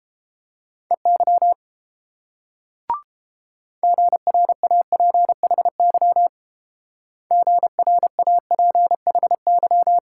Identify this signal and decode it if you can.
Morse code, heard as a keyed tone, about 28 wpm, 700 Hz. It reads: EY GRAPHY GRAPHY